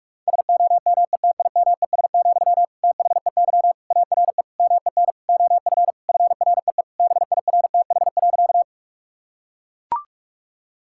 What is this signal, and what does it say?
Morse code, heard as a keyed tone, about 42 wpm, 700 Hz. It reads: SOMETIMES, THEY ARE MEN OF FREE BIRTH.
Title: Sentences from Top 1000 Words - Review Part 5 - 45wpm